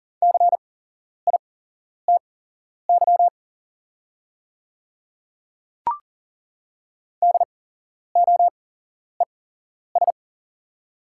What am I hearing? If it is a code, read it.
Morse code, heard as a keyed tone, about 40 wpm, spaced as for 11 wpm, 700 Hz. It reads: CITY DOES
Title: 8x Character Spacing - Top 300 Words - 40wpm